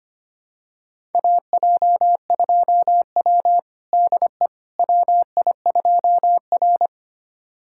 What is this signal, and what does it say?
Morse code, heard as a keyed tone, about 25 wpm, 700 Hz. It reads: AJ2W DE WI2R
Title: USA Callsign DE Callsign - 4 Characters - 25wpm